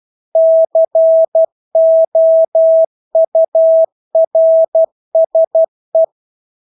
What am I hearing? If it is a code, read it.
Morse code, heard as a keyed tone, about 12 wpm, 650 Hz. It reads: COURSE
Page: Engelske ord 12 wpm